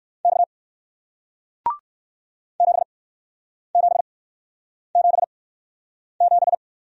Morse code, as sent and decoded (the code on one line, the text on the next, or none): -..- / --... --... --... --...
X 7777